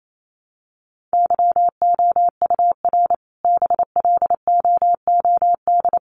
Morse code, heard as a keyed tone, about 28 wpm, 700 Hz. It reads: YOUR BLOOD
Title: Sentences from Top 1100 Words 28wpm